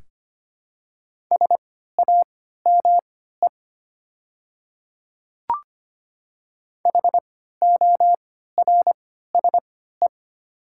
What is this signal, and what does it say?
Morse code, heard as a keyed tone, about 25 wpm, spaced as for 14 wpm, 700 Hz. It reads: SAME HORSE
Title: 3x Character Spacing - Top 300 Words - 25wpm